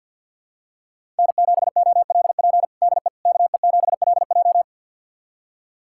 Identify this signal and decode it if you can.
Morse code, heard as a keyed tone, about 47 wpm, 700 Hz. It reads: N7OLP DE KE7LJ
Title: USA Callsign DE Callsign - 5 Characters - 50wpm